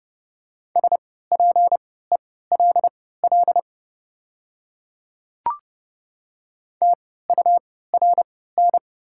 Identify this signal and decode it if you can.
Morse code, heard as a keyed tone, about 30 wpm, spaced as for 17 wpm, 690 Hz. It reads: SPELL TURN
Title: 3x Character Spacing - Top 200 Words - 30wpm